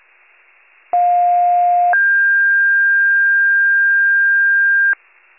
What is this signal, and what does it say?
Two-tone page, heard as a one-second tone, then a three-second tone, about 710 then 1600 Hz. It is heard from 0.9 to 4.9 s.